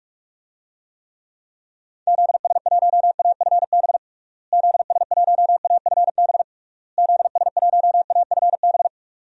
Morse code, heard as a keyed tone, about 46 wpm, 700 Hz. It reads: ZS1AFB ZS1AFB ZS1AFB
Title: DXCC-100 Call Signs - 6 Characters - Repeated 3x - 45wpm